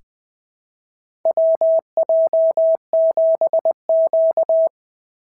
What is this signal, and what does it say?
Morse code, heard as a keyed tone, about 21 wpm, 690 Hz. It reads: WJ7Q
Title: SOTA QSO Practice 20wpm